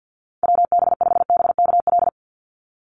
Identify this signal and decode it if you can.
Morse code, heard as a keyed tone, about 50 wpm, 700 Hz. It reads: WB5BKL
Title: NA SOTA Chaser Callsigns 50wpm